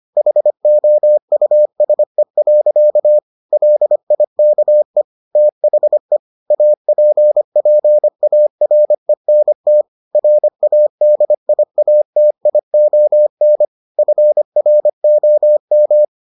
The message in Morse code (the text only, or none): HOUSE. LIKE THE APPARENT RADIATION FROM